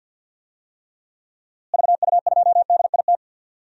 Morse code, heard as a keyed tone, about 45 wpm, 700 Hz. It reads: VU2DIT
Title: DXCC-100 Call Signs - 6 Characters - Repeated 3x - 50wpm